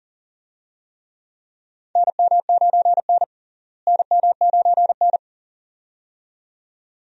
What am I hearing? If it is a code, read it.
Morse code, heard as a keyed tone, about 40 wpm, 700 Hz. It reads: NM9N NM9N
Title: USA Call Signs - 4 Characters - Repeated 2x - 40wpm